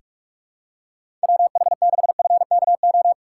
Morse code, heard as a keyed tone, about 46 wpm, 700 Hz. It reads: WH6FKO
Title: USA Call Signs - 6 Characters - 6x Character Spacing - 45wpm